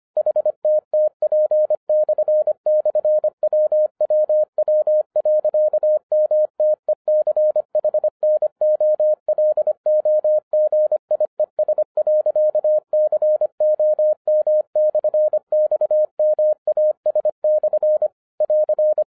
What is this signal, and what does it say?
Morse code, heard as a keyed tone, about 25 wpm, 610 Hz. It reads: HTTP//WWW.MTECHNOLOGIES.COM/XMAS/ <AR>
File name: 2011-10-mp3 / n14-MORSE EXPRESS 2011 CHRISTMAS KEY 25wpm